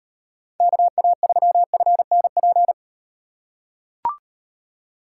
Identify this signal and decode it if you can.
Morse code, heard as a keyed tone, about 38 wpm, 700 Hz. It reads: KA3FNP